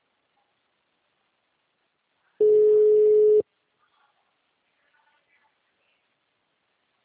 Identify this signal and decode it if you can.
Ringback tone (Europe, 425 Hz).